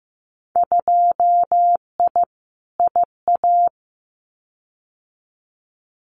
Morse code, heard as a keyed tone, about 15 wpm, 700 Hz. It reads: 2I IA
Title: Winter Field Day Exchanges Encore 15wpm